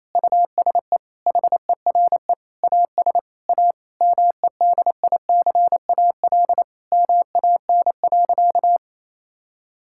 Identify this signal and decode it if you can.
Morse code, heard as a keyed tone, about 28 wpm, 700 Hz. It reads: USE HERE AS A MEDICAL MAN.
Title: Sentences from Top 1000 Words - Review Part 5 - 28wpm